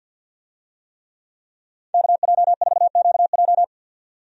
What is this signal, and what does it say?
Morse code, heard as a keyed tone, about 52 wpm, 700 Hz. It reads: KJ4QJ
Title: USA Call Signs - 5 Characters - 50wpm